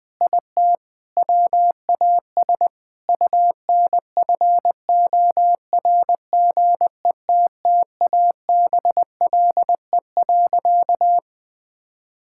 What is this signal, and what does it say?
Morse code, heard as a keyed tone, about 20 wpm, 700 Hz. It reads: IT WAS UNFORGETTABLE.
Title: Sentences with 6-Word Phrases - 20wpm